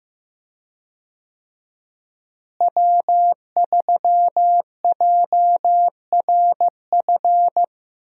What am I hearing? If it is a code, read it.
Morse code, heard as a keyed tone, about 15 wpm, 700 Hz. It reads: W3JRF
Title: USA Call Signs - 5 Characters - Repeated 2x - 15wpm